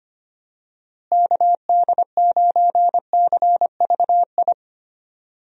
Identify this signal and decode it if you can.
Morse code, heard as a keyed tone, about 25 wpm, 700 Hz. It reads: KD9CVI
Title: USA Call Signs - 6 Characters - Repeated 3x - 25wpm